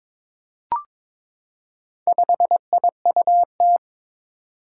Morse code, heard as a keyed tone, about 22 wpm, 700 Hz. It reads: 5IUT
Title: Four Challenging Letter-Number Combinations 22wpm